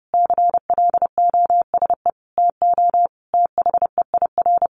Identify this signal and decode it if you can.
Morse code, heard as a keyed tone, about 28 wpm, 700 Hz. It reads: CLOSE TO THEIR
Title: Sentences from Top 900 Words 30wpm